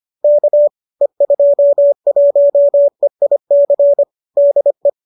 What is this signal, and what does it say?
Morse code, heard as a keyed tone, about 25 wpm, 570 Hz. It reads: K E21EIC DE
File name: qso208 25